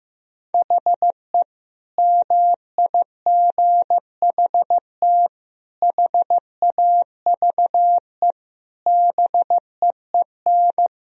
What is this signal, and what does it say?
Morse code, heard as a keyed tone, about 15 wpm, 700 Hz. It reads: HE MIGHT HAVE BEEN
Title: Sentences from Top 1000 Words - Review Part 2 - 15wpm